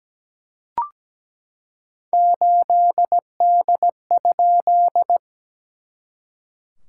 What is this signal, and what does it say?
Morse code, heard as a keyed tone, about 17 wpm, 690 Hz. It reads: 8D?
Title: SOTA QSO Practice 17wpm